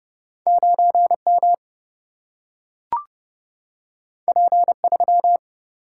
Morse code, heard as a keyed tone, about 30 wpm, 700 Hz. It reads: 9 M P 3
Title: Single-Letter Single-Number 30wpm